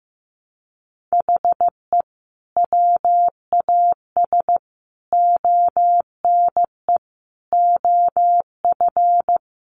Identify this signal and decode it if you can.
Morse code, heard as a keyed tone, about 15 wpm, 700 Hz. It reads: HE WAS ONE OF
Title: Sentences from Top 1300 Words 15wpm